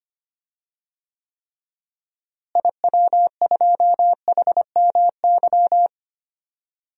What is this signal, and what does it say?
Morse code, heard as a keyed tone, about 25 wpm, 700 Hz. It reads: IW2HMY
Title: DXCC-100 Call Signs - 6 Characters - Repeated 2x - 25wpm